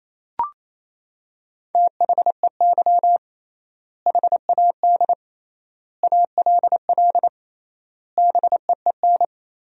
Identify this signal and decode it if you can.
Morse code, heard as a keyed tone, about 27 wpm, 700 Hz. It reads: THEY HAD ALL BEEN
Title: 3x Word-Spacing - Sentences from Top 200 Words - 28wpm